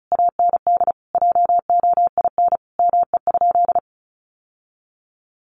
Morse code, heard as a keyed tone, about 34 wpm, 700 Hz. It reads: AND JOIN ME?
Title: Sentences from Top 1400 Words 35wpm